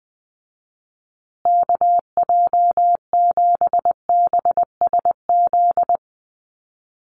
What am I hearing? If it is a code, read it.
Morse code, heard as a keyed tone, about 20 wpm, 700 Hz. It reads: KJ7BSZ
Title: USA Call Signs Repeated 2x 20wpm